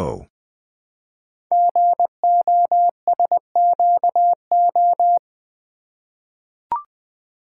Morse code, heard as a keyed tone, about 20 wpm, 700 Hz. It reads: GOSQO
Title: Code Groups - 5 Letters - 20wpm